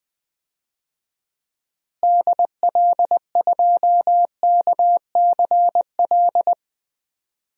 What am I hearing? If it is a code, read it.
Morse code, heard as a keyed tone, about 20 wpm, 700 Hz. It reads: DL2KCL
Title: DXCC-100 Call Signs - 6 Characters - Repeated 3x - 20wpm